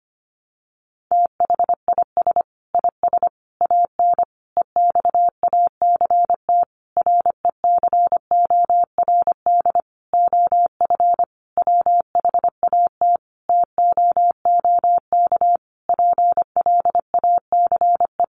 Morse code, heard as a keyed tone, about 25 wpm, 700 Hz. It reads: THIS IS AN EXACT RECORD OF WHAT TOOK PLACE
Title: Sentences from Top 1300 Words 25wpm